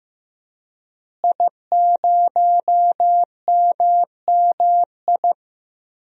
Morse code, heard as a keyed tone, about 15 wpm, 700 Hz. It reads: I0MMI
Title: DXCC-100 Call Signs - 5 Characters - Repeated 3x - 15wpm